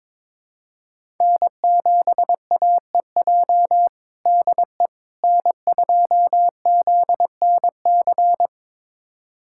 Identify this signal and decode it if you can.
Morse code, heard as a keyed tone, about 22 wpm, 700 Hz. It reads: N7AEJ DE N2ZNC
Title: USA Callsign DE Callsign 22wpm